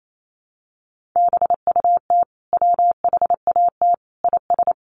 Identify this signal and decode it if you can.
Morse code, heard as a keyed tone, about 28 wpm, 700 Hz. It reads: BUT WHAT IS
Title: Sentences from Top 1500 Words 28wpm